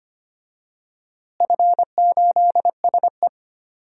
Morse code, heard as a keyed tone, about 25 wpm, 700 Hz. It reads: F8SE
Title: DXCC-100 Call Signs - 4 Characters - Repeated 2x - 25wpm